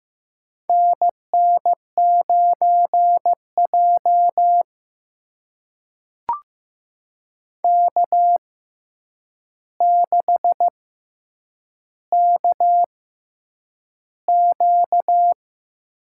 Morse code, heard as a keyed tone, about 15 wpm, spaced as for 8 wpm, 700 Hz. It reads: NN9J K6KQ
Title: USA Call Signs - 4 Characters - 6x Character Spacing - 15wpm